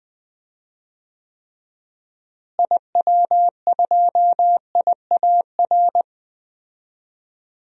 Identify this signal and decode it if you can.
Morse code, heard as a keyed tone, about 20 wpm, 700 Hz. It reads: IW2IAR
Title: DXCC-100 Call Signs - 6 Characters - Repeated 2x - 20wpm